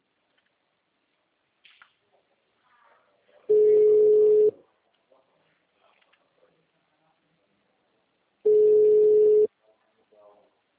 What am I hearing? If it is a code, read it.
Ringback tone (Europe, 425 Hz), with speech.
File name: out- -2006- - - .46418